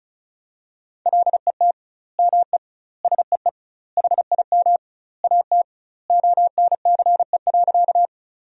Morse code, heard as a keyed tone, about 36 wpm, 700 Hz. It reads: LET ME SEE HIM AT ONCE.
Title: 2x Word-Spacing - Sentences from Top 300 Words - 35wpm